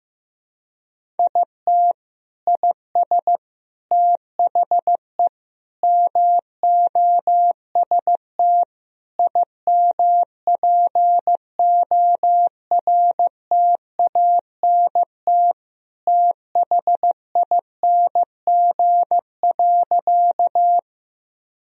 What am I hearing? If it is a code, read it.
Morse code, heard as a keyed tone, about 15 wpm, 700 Hz. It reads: IT IS THE MOST IMPORTANT THING.
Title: Sentences from Top 1000 Words - Review Part 1 - 15wpm